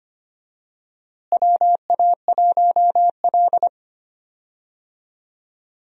Morse code, heard as a keyed tone, about 25 wpm, 700 Hz. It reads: WA1L
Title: USA Call Signs - 4 Characters - Repeated 3x - 25wpm